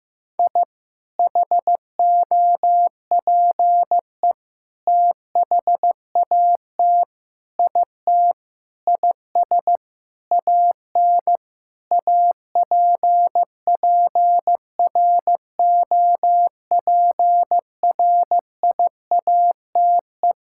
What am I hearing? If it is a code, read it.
Morse code, heard as a keyed tone, about 15 wpm, 700 Hz. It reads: I HOPE THAT IT IS AN APPROPRIATE
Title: Sentences from Top 1000 Words - Review Part 5 - 15wpm